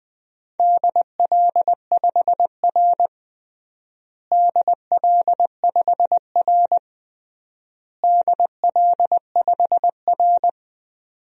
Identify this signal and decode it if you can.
Morse code, heard as a keyed tone, about 20 wpm, 700 Hz. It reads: DL5R DL5R DL5R
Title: DXCC-100 Call Signs - 4 Characters - Repeated 3x - 20wpm